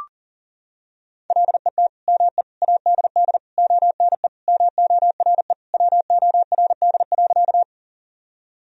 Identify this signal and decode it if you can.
Morse code, heard as a keyed tone, about 40 wpm, 700 Hz. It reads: LET ME ADD ONE MORE WORD.
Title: Sentences from Top 1000 Words - Review Part 1 - 40wpm